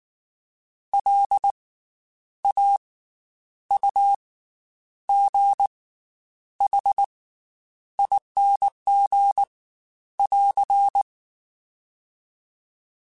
Morse code, heard as a keyed tone, about 19 wpm, 800 Hz. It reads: L A U G H ING <AR>
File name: Special INGs / Special INGs-mp3s / Slow ing @18